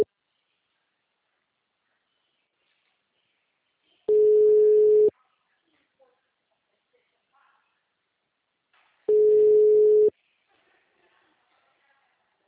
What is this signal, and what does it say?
Ringback tone (Europe, 425 Hz).